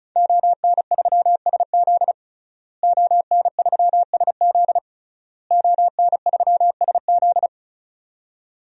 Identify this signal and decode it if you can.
Morse code, heard as a keyed tone, about 36 wpm, 700 Hz. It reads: ON3SZ ON3SZ ON3SZ
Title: DXCC-100 Call Signs - 5 Characters - Repeated 3x - 35wpm